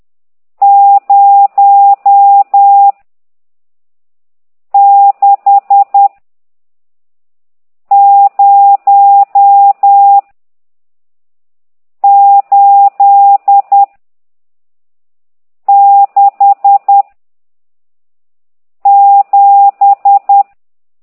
Morse code, heard as a keyed tone, about 10 wpm, 800 Hz. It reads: 0 6 0 8 6 7
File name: ct2idw / exerc14